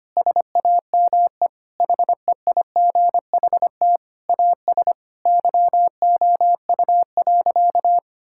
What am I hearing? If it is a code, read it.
Morse code, heard as a keyed tone, about 25 wpm, 700 Hz. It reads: SAME HEIGHT AS YOU.
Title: Sentences from Top 1600 Words 25wpm